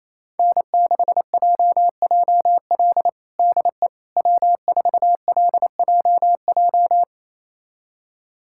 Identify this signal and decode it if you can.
Morse code, heard as a keyed tone, about 28 wpm, 700 Hz. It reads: N6JJL DE W4LJJ
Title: USA Callsign DE Callsign - 5 Characters - 28wpm